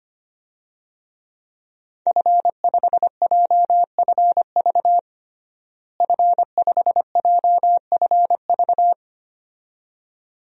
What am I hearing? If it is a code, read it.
Morse code, heard as a keyed tone, about 25 wpm, 700 Hz. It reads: F5JFV F5JFV
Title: DXCC-100 Call Signs - 5 Characters - Repeated 2x - 25wpm